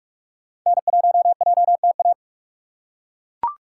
Morse code, heard as a keyed tone, about 41 wpm, 700 Hz. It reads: N1JTA
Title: USA Call Signs - 5 Characters - 6x Character Spacing - 45wpm